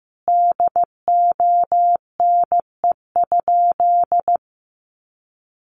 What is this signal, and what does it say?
Morse code, heard as a keyed tone, about 15 wpm, 700 Hz. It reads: DONE?
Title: Sentences from Top 500 Words 15wpm